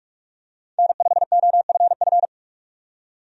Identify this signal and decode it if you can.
Morse code, heard as a keyed tone, about 46 wpm, 700 Hz. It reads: N5OFF
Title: USA Call Signs - 5 Characters - 45wpm